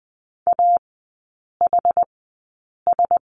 Morse code, heard as a keyed tone, about 20 wpm, spaced as for 8 wpm, 700 Hz. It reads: AHS
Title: 081 Lesson 14e - Number 2 - Three Characters 20wpm